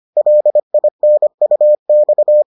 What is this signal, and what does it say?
Morse code, heard as a keyed tone, about 25 wpm, 600 Hz. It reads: LINUX